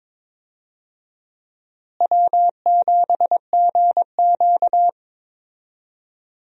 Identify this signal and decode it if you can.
Morse code, heard as a keyed tone, about 21 wpm, 700 Hz. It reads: W7GQ
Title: USA Call Signs - 4 Characters - Repeated 3x - 22wpm